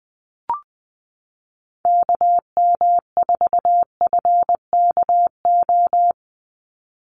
Morse code, heard as a keyed tone, about 20 wpm, 700 Hz. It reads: KM4FKO
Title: USA Call Signs 20wpm